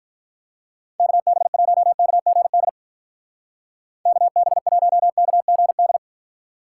Morse code, heard as a keyed tone, about 47 wpm, 700 Hz. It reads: KB1KGD KB1KGD